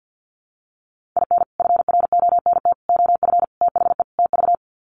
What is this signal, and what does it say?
Morse code, heard as a keyed tone, about 50 wpm, 700 Hz. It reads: IN FRONT OF THE TV
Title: Sets of 5 Words Encore2 50wpm